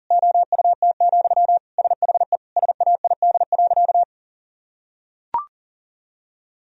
Morse code, heard as a keyed tone, about 40 wpm, 700 Hz. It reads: OUT, SHE SAID.
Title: Sentences from Top 1000 Words - Review Part 3 - 40wpm